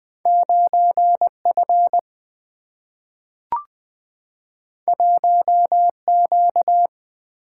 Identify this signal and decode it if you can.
Morse code, heard as a keyed tone, about 20 wpm, 700 Hz. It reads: 9 F 1 Q
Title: Two Letter-Number Combinations - 20wpm